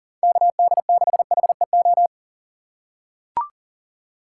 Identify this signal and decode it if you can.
Morse code, heard as a keyed tone, about 40 wpm, 700 Hz. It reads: KD6HEO
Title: USA Call Signs - 6 Characters - 3x Character Spacing - 40wpm